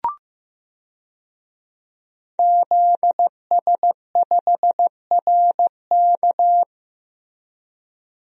Morse code, heard as a keyed tone, about 15 wpm, 700 Hz. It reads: ZS5RK
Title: DXCC-100 Call Signs - 5 Characters - Repeated 2x - 15wpm